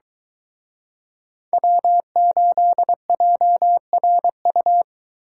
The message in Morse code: W8JRU

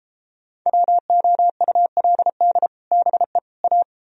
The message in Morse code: WOULD BE A

